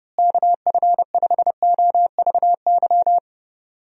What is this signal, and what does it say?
Morse code, heard as a keyed tone, about 32 wpm, 700 Hz. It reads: KF5OVY
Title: USA Call Signs - 6 Characters - Repeated 2x - 30wpm